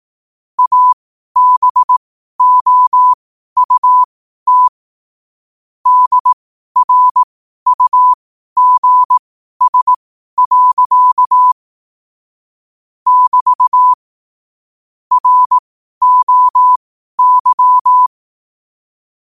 Morse code, heard as a keyed tone, about 18 wpm, spaced as for 12 wpm, 1000 Hz. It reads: ABOUT DRUGS. <BT> ROY